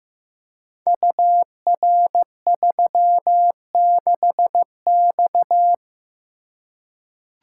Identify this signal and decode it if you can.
Morse code, heard as a keyed tone, about 15 wpm, 700 Hz. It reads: UR36X
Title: Code Groups - 5 Letter:Numbers - 15wpm